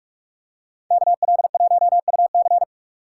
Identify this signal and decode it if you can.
Morse code, heard as a keyed tone, about 44 wpm, 700 Hz. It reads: KL1UC